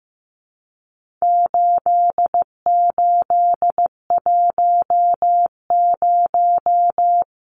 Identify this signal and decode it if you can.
Morse code, heard as a keyed tone, about 15 wpm, 700 Hz. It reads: 8810
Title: Numbers - 4-Digit 15wpm